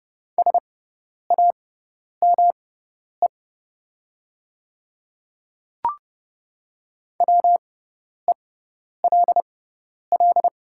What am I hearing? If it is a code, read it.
Morse code, heard as a keyed tone, about 30 wpm, spaced as for 10 wpm, 700 Hz. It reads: SAME WELL
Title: 6x Character Spacing - Top 300 Words - 30wpm